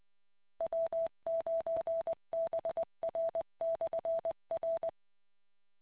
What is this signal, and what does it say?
Morse code, heard as a keyed tone, about 20 wpm, 660 Hz. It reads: W9BR/R